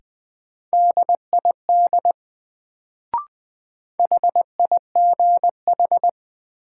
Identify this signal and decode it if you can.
Morse code, heard as a keyed tone, about 20 wpm, 700 Hz. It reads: DID HIGH
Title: Sets of 2 Words - New Word Drill - 20wpm